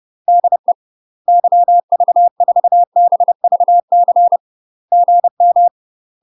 Morse code, heard as a keyed tone, about 30 wpm, 700 Hz. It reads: DE YV4BVC GM
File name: UK MASTER QSOs TRAINING / 30WPM / UK MASTER QSOs 0016